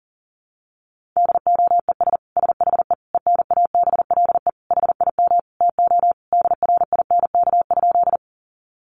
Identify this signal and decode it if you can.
Morse code, heard as a keyed tone, about 40 wpm, 700 Hz. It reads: DOES SHE ENABLE HIM TO DRINK?
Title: Sentences from Top 1500 Words 40wpm